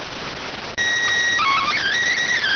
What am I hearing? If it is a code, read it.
SSTV (slow-scan television), PD 290.